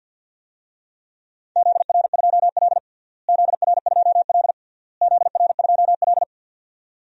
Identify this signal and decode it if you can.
Morse code, heard as a keyed tone, about 52 wpm, 700 Hz. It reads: ZR2L ZR2L ZR2L
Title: DXCC-100 Call Signs - 4 Characters - Repeated 3x - 50wpm